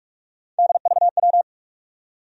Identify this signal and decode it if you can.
Morse code, heard as a keyed tone, about 48 wpm, 700 Hz. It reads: DVW
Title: USA Call Sign Postfixes - 3 Letters 45wpm